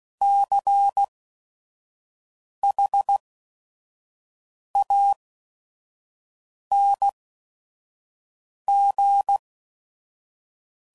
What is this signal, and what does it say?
Morse code, heard as a keyed tone, about 16 wpm, 800 Hz. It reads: C H A N G